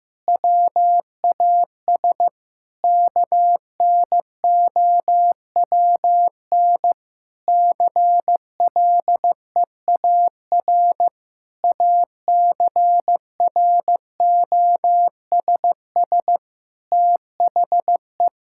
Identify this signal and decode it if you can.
Morse code, heard as a keyed tone, about 15 wpm, 700 Hz. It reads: WAS KNOWN CLEAR ACROSS THE
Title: Sentences from Top 1000 Words - Review Part 5 - 15wpm